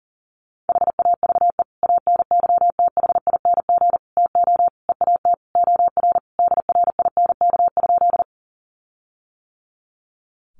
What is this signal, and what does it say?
Morse code, heard as a keyed tone, about 40 wpm, 700 Hz. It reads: HAVE ANYTHING TO EAT OR DRINK?
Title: Sentences from Top 1500 Words 40wpm